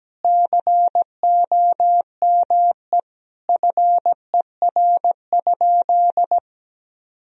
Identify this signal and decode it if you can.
Morse code, heard as a keyed tone, about 17 wpm, 700 Hz. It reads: COME FER?
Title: Sentences from Top 100 Words and Abbreviations 17wpm